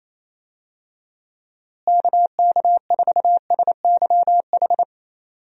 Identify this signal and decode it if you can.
Morse code, heard as a keyed tone, about 28 wpm, 700 Hz. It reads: KK4SYH
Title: USA Call Signs - 6 Characters - Repeated 3x - 28wpm